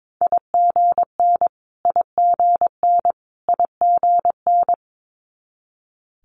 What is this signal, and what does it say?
Morse code, heard as a keyed tone, about 22 wpm, 700 Hz. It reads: IGN IGN IGN
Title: Multi-letter Phonemes Repeated 3x 22wpm